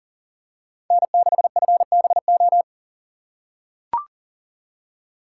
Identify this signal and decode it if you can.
Morse code, heard as a keyed tone, about 40 wpm, 700 Hz. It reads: N6FBO
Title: USA Call Signs - 5 Characters - 3x Character Spacing - 40wpm